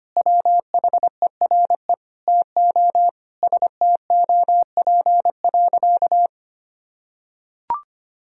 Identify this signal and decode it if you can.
Morse code, heard as a keyed tone, about 25 wpm, 700 Hz. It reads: WHERE TO STOP.
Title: Sentences from Top 1000 Words - Review Part 2 - 25wpm